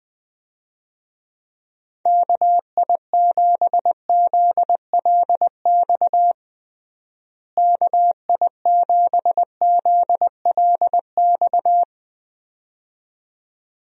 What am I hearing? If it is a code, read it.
Morse code, heard as a keyed tone, about 21 wpm, 700 Hz. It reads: KI7ZLX KI7ZLX
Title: USA Call Signs - 6 Characters - Repeated 2x - 20wpm